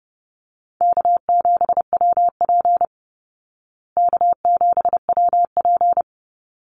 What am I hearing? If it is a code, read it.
Morse code, heard as a keyed tone, about 30 wpm, 700 Hz. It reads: K7WP K7WP
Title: Easy Call Signs Repeated 2x - 30wpm